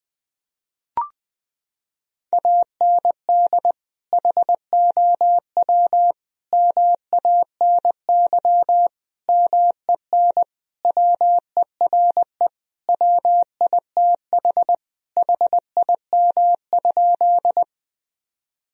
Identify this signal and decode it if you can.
Morse code, heard as a keyed tone, about 20 wpm, 700 Hz. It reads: AND HOW MANY MEN WERE WITH HIM?
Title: Sentences from Top 1000 Words - Review Part 1 - 20wpm